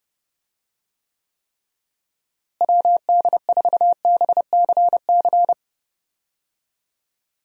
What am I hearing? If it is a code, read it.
Morse code, heard as a keyed tone, about 32 wpm, 700 Hz. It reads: WD4BCC
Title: USA Call Signs - 6 Characters - Repeated 3x - 30wpm